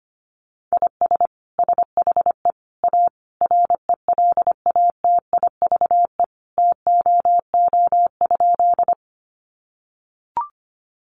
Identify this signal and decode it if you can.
Morse code, heard as a keyed tone, about 25 wpm, 700 Hz. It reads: IS SHE A RELATIVE TOO?
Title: Sentences from Top 1400 Words 25wpm